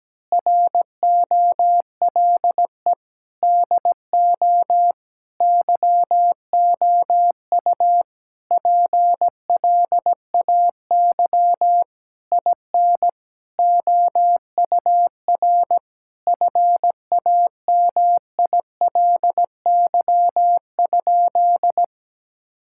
Morse code, heard as a keyed tone, about 17 wpm, 700 Hz. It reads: ROLE DO YOU PLAY IN OUR FAMILY?
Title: Sentences from Top 1000 Words - Review Part 6 - 17wpm